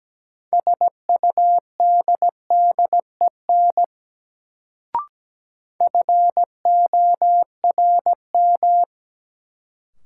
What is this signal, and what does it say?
Morse code, heard as a keyed tone, about 17 wpm, 700 Hz. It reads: SUDDEN FORM
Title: Sets of 4 Words - New Word Drill - 17wpm